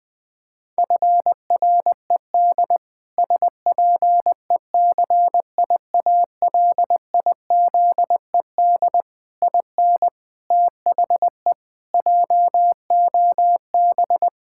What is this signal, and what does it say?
Morse code, heard as a keyed tone, about 20 wpm, 700 Hz. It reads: FRED SPECIALIZED IN THE JOB
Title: Pangram Challenge v2 - 20wpm